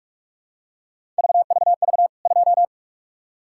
Morse code, heard as a keyed tone, about 46 wpm, 700 Hz. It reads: VVV 2
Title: K1USN SST QSO Practice 45wpm